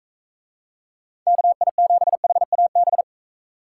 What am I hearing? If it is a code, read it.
Morse code, heard as a keyed tone, about 42 wpm, 700 Hz. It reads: KI7HAB